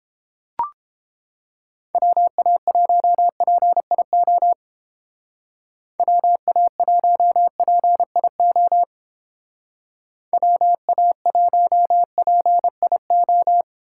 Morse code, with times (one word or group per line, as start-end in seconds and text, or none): 1.9-4.5 WA1PIO
6.0-8.8 WA1PIO
10.3-13.6 WA1PIO